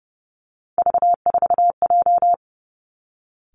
Morse code, heard as a keyed tone, about 30 wpm, 700 Hz. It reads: V4J
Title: Three Challenging Letter-Number Combinations 30wpm